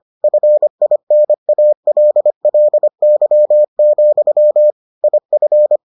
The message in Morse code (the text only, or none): FINALLY, IF